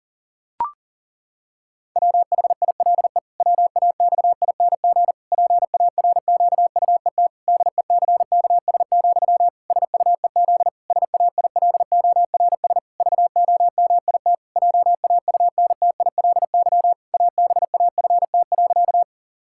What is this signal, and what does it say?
Morse code, heard as a keyed tone, about 40 wpm, 700 Hz. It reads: WHILE WAXING PARQUET DECKS, SUEZ SAILORS VOMIT JAUNTILY ABAFT.
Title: Pangram Challenge v2 - 40wpm